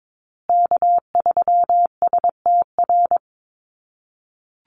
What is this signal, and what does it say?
Morse code, heard as a keyed tone, about 22 wpm, 700 Hz. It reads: K3STR
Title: USA Call Signs Repeated 2x 22wpm